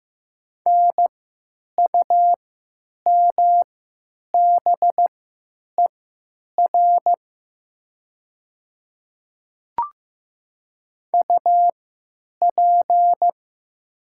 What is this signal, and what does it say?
Morse code, heard as a keyed tone, about 15 wpm, spaced as for 8 wpm, 700 Hz. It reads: NUMBER UP
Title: 3x Character Spacing - Top 100 Words - 15wpm